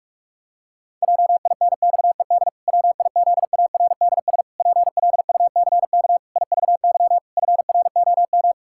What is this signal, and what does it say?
Morse code, heard as a keyed tone, about 46 wpm, 700 Hz. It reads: JINXED WIZARDS PLUCK IVY FROM
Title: Pangram Challenge v2 - 45wpm